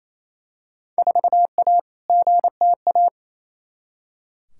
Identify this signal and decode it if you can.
Morse code, heard as a keyed tone, about 28 wpm, 700 Hz. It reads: 4A GTA
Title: Field Day Exchanges Encore - 28wpm